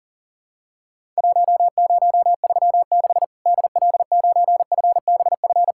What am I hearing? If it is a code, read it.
Morse code, heard as a keyed tone, about 40 wpm, 700 Hz. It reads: 1036 DL9FBF
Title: Worked All Europe DX Contest - QTC Exchange - 40wpm